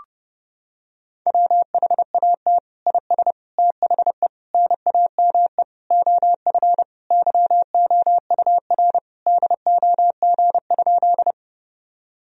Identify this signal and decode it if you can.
Morse code, heard as a keyed tone, about 31 wpm, 700 Hz. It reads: WHAT IS THE NAME OF YOUR DOG?
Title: Sentences from Top 1000 Words - Review Part 2 - 30wpm